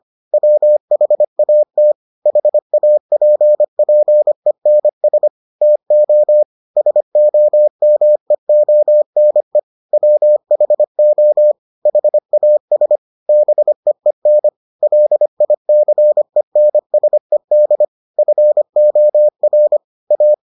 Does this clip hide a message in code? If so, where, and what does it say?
Yes: Morse code from 0.3 to 20.3 s, heard as a keyed tone, about 25 wpm, 600 Hz. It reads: WHAT HAPPENS TO SOMEONE WHO HAS BEEN LICENSED FOR A